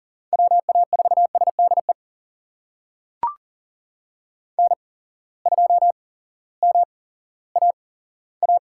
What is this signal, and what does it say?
Morse code, heard as a keyed tone, about 40 wpm, spaced as for 18 wpm, 700 Hz. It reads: WA4SDE N2MAA
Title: USA Call Signs - 8x Character Spacing - 40wpm